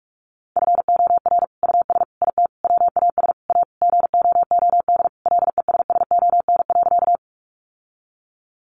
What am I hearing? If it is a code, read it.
Morse code, heard as a keyed tone, about 44 wpm, 700 Hz. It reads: FOR US IT WAS A GOOD LESSON.
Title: Sentences from Top 1200 Words 45wpm